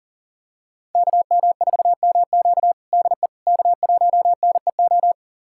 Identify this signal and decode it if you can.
Morse code, heard as a keyed tone, about 40 wpm, 700 Hz. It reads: KM4MQ DE K1NEO
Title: USA Callsign DE Callsign - 5 Characters - 40wpm